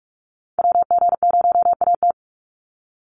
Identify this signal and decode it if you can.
Morse code, heard as a keyed tone, about 44 wpm, 700 Hz. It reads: WG0AT
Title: NA SOTA Chaser Callsigns 45wpm